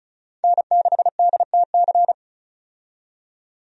Morse code, heard as a keyed tone, about 36 wpm, 700 Hz. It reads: N6DTC
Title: USA Call Signs - 5 Characters - 35wpm